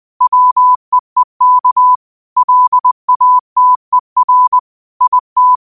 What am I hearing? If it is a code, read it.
Morse code, heard as a keyed tone, about 20 wpm, 1000 Hz. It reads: WEEK LATER IT
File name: qotdfiles / 20wpm